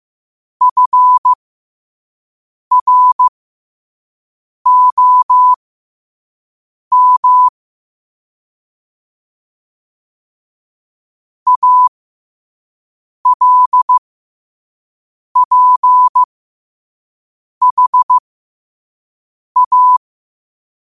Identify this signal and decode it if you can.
Morse code, heard as a keyed tone, about 15 wpm, spaced as for 5 wpm, 1000 Hz. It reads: FROM ALPHA
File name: qotdfiles / 5wpm